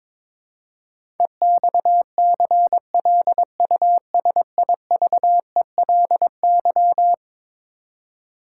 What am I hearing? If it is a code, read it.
Morse code, heard as a keyed tone, about 22 wpm, 700 Hz. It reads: EXCLUSIVELY
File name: practice / 22wpm / Eleven Letter Words - 22wpm